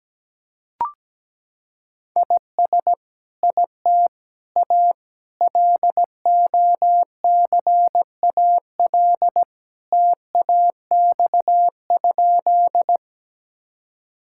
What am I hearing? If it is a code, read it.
Morse code, heard as a keyed tone, about 17 wpm, 700 Hz. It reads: IS IT A LOCAL TAX?
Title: Sentences from Top 1000 Words - Review Part 5 - 17wpm